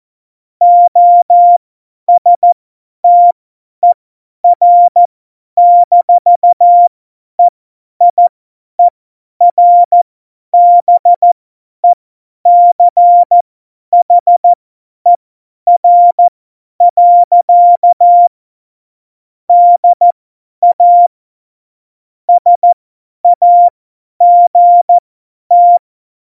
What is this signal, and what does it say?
Morse code, heard as a keyed tone, about 14 wpm, spaced as for 10 wpm, 700 Hz. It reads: OSTER-EIERBECHER. DA SAGT
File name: Morsekurs / Witze / Witz 14 April 2020, 50 F70